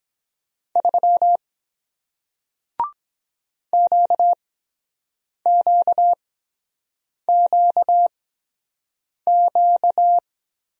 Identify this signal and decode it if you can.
Morse code, heard as a keyed tone, about 23 wpm, spaced as for 7 wpm, 700 Hz. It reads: 3 QQQQ